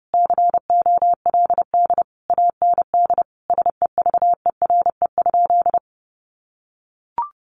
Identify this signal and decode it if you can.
Morse code, heard as a keyed tone, about 30 wpm, 700 Hz. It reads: COLD AND SEVERE?
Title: Sentences from Top 1400 Words 30wpm